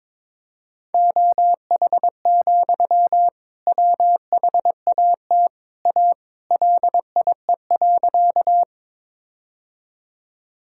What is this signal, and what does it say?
Morse code, heard as a keyed tone, about 22 wpm, 700 Hz. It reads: OH, WHAT A LIE.
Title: Sentences from Top 1600 Words 22wpm